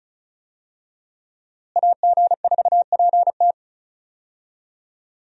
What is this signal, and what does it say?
Morse code, heard as a keyed tone, about 34 wpm, 700 Hz. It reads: AG4PT
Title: USA Call Signs - 5 Characters - Repeated 2x - 35wpm